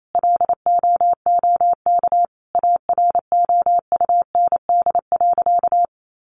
Morse code, heard as a keyed tone, about 28 wpm, 700 Hz. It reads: LOOK AROUND.
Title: Sentences from Top 600 Words 28wpm